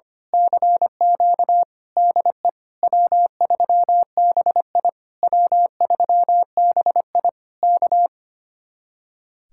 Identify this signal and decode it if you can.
Morse code, heard as a keyed tone, about 24 wpm, 700 Hz. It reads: CQ DE W3BI W3BI K
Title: Speed-Racing Common QSO Phrases - 25wpm